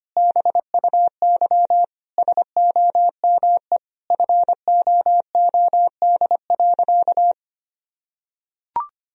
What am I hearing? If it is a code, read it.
Morse code, heard as a keyed tone, about 25 wpm, 700 Hz. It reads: BUY SOME FOOD.
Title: Sentences from Top 1000 Words - Review Part 4 - 25wpm